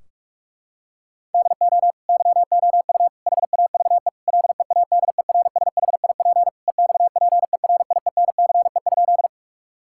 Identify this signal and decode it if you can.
Morse code, heard as a keyed tone, about 46 wpm, 700 Hz. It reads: DO YOU HAVE LEADERSHIP EXPERIENCE?
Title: Sentences from Top 1600 Words 45wpm